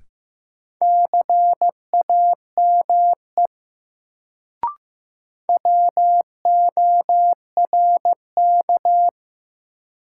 Morse code, heard as a keyed tone, about 15 wpm, 700 Hz. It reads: CAME WORK
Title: Sentences from Top 200 Words - New Word Drill - 15wpm